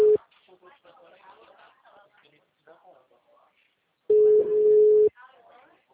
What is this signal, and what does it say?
Ringback tone (Europe, 425 Hz), with speech.